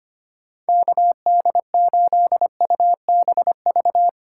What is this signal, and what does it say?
Morse code, heard as a keyed tone, about 25 wpm, 700 Hz. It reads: KD8UBV
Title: USA Call Signs - 6 Characters - 8x Character Spacing - 25wpm